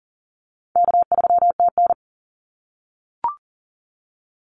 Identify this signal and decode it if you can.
Morse code, heard as a keyed tone, about 38 wpm, 700 Hz. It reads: K3TN
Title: Call Signs - 40wpm